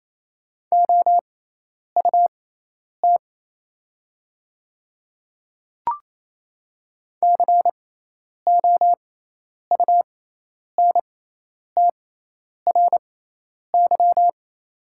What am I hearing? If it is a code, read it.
Morse code, heard as a keyed tone, about 27 wpm, spaced as for 10 wpm, 690 Hz. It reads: OUT COUNTRY
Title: 6x Character Spacing - Top 300 Words - 28wpm